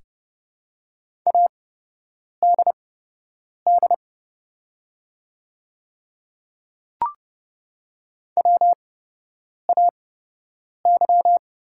Morse code, heard as a keyed tone, about 30 wpm, spaced as for 8 wpm, 700 Hz. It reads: ADD WAY
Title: 8x Character Spacing - Top 200 Words - 30wpm